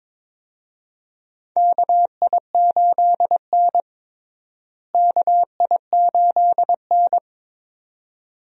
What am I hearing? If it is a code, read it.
Morse code, heard as a keyed tone, about 21 wpm, 700 Hz. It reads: KI8N KI8N
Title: USA Call Signs - 4 Characters - Repeated 2x - 22wpm